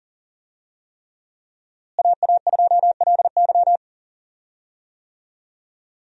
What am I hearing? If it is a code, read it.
Morse code, heard as a keyed tone, about 40 wpm, 700 Hz. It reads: AA2LY
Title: USA Call Signs - 5 Characters - Repeated 3x - 40wpm